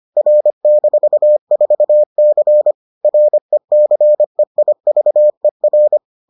Morse code, heard as a keyed tone, about 25 wpm, 600 Hz. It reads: R-4C RECEIVER